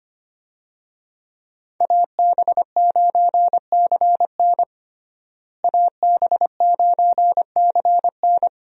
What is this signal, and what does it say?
Morse code, heard as a keyed tone, about 25 wpm, 700 Hz. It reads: AB9CN AB9CN
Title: USA Call Signs - 5 Characters - Repeated 3x - 25wpm